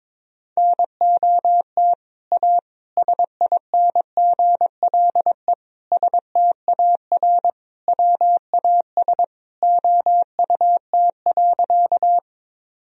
Morse code, heard as a keyed tone, about 22 wpm, 700 Hz. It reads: NOT A SINGLE STAR WAS OUT.
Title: Sentences from Top 1000 Words - Review Part 4 - 22wpm